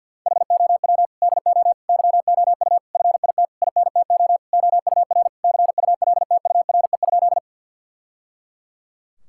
Morse code, heard as a keyed tone, about 50 wpm, 700 Hz. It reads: HOW DO YOU FIT INTO OUR CULTURE?
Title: Sentences from Top 1000 Words - Review Part 6 - 50wpm